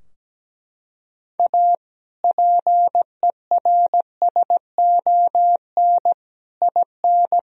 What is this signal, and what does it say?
Morse code, heard as a keyed tone, about 17 wpm, 700 Hz. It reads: A PERSON IN
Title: Sentences from Top 1000 Words - Review Part 4 - 17wpm